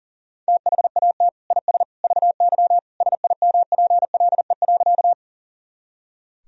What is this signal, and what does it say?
Morse code, heard as a keyed tone, about 40 wpm, 700 Hz. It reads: THAT IS VY SIMPLE.
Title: Sentences from Top 400 Words and Abbreviations 40wpm